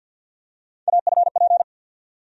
Morse code, heard as a keyed tone, about 50 wpm, 700 Hz. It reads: AFP
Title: USA Call Sign Postfixes - 3 Letters 50wpm